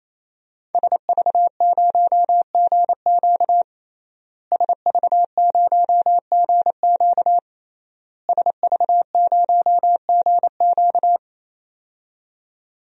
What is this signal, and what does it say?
Morse code, heard as a keyed tone, about 28 wpm, 700 Hz. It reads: SV0GQ SV0GQ SV0GQ
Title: DXCC-100 Call Signs - 5 Characters - Repeated 3x - 28wpm